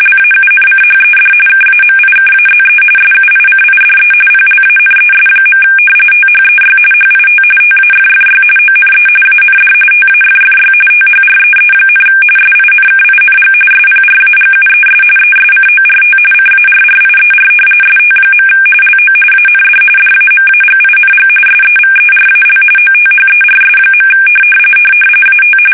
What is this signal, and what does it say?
RTTY, about 45.45 baud. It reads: LAUNCHED KAMIKAZE SUICIDE PLANES. THEY HIT TAFFY ONE AND TAFFY THREE, SINKING SPRAGUES ESCORT CARRIER ST. LO AND DAMAGING SEVERAL OTHERS. AT
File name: fsk850 / battleshipnj